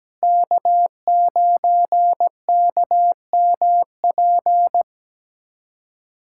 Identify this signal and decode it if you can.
Morse code, heard as a keyed tone, about 17 wpm, 700 Hz. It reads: K9KMP
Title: USA Callsign DE Callsign 17wpm